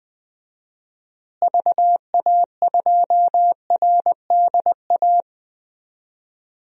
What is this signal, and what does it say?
Morse code, heard as a keyed tone, about 20 wpm, 700 Hz. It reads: VA2RDA
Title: DXCC-100 Call Signs - 6 Characters - Repeated 3x - 20wpm